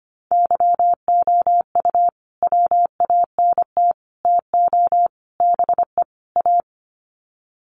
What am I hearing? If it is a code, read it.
Morse code, heard as a keyed tone, about 25 wpm, 700 Hz. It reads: YOU WANT TO BE A
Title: Sets of 5 Words Encore2 25wpm